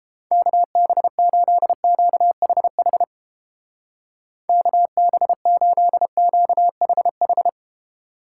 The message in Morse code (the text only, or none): KB8QHH KB8QHH